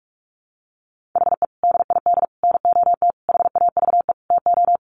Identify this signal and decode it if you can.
Morse code, heard as a keyed tone, about 44 wpm, 700 Hz. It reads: HE DID NOT HAVE TO
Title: Sets of 5 Words 45wpm